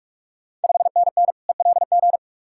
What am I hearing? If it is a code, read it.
Morse code, heard as a keyed tone, about 46 wpm, 700 Hz. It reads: 5NN ELG
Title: Ontario QSO Party Exchange 45wpm